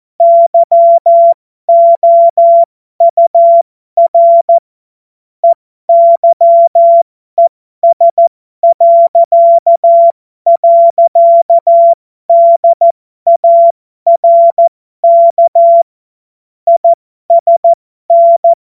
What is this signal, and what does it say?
Morse code, heard as a keyed tone, about 14 wpm, spaced as for 12 wpm, 680 Hz. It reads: YOUR EYES..DARK ISN
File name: Morsekurs / Witze